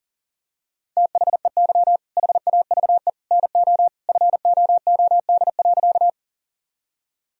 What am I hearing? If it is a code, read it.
Morse code, heard as a keyed tone, about 40 wpm, 700 Hz. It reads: THEY HAVE NO FOOD.
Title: Sentences from Top 1000 Words - Review Part 1 - 40wpm